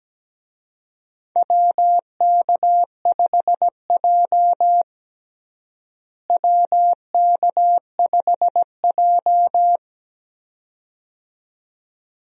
Morse code, heard as a keyed tone, about 17 wpm, 700 Hz. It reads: WK5J WK5J
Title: USA Call Signs - 4 Characters - Repeated 2x - 17wpm